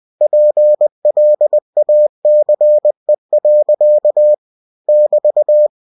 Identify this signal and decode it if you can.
Morse code, heard as a keyed tone, about 20 wpm, 600 Hz. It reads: PLACE. <BT>
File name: cw-books / winnie-the-pooh / winnie-20wpm-page-0081